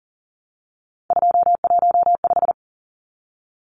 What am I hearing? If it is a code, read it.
Morse code, heard as a keyed tone, about 40 wpm, 700 Hz. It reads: 215
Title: Numbers - 3-Digit 40wpm